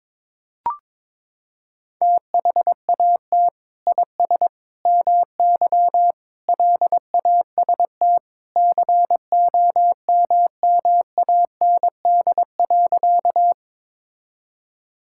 Morse code, heard as a keyed tone, about 22 wpm, 700 Hz. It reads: THAT IS MY LAST COMMAND.
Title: Sentences from Top 1600 Words 22wpm